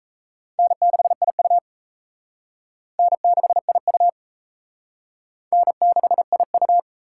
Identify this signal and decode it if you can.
Morse code, heard as a keyed tone, about 37 wpm, 700 Hz. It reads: N6IU N6IU N6IU